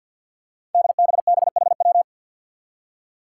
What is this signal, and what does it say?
Morse code, heard as a keyed tone, about 53 wpm, 700 Hz. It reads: DBBHW
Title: Five Challenging Letter-Number Combinations 50wpm